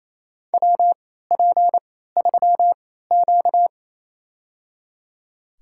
Morse code, heard as a keyed tone, about 28 wpm, spaced as for 16 wpm, 700 Hz. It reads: WP3Q
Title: USA Call Signs - 4 Characters - 3x Character Spacing - 28wpm